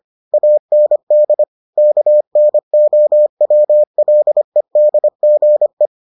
Morse code, heard as a keyed tone, about 25 wpm, 600 Hz. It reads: AND KNOWLEDGE